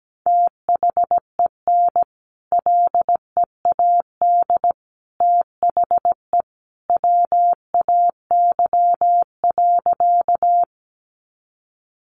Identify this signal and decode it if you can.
Morse code, heard as a keyed tone, about 17 wpm, 700 Hz. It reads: THEN LEAD THE WAY.
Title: Sentences from Top 500 Words 17wpm